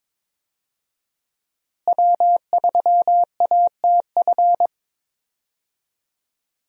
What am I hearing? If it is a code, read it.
Morse code, heard as a keyed tone, about 21 wpm, 700 Hz. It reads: W3ATF
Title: USA Call Signs - 5 Characters - Repeated 2x - 22wpm